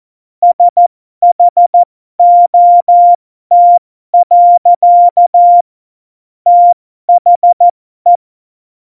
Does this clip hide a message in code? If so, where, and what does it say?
Yes: Morse code from 0.4 to 8.2 s, heard as a keyed tone, about 14 wpm, spaced as for 12 wpm, 700 Hz. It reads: SHOT. THE